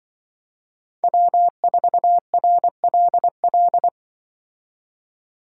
Morse code, heard as a keyed tone, about 23 wpm, 700 Hz. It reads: W4RLL
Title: Speed Racing - USA 1x3 Call Signs 20wpm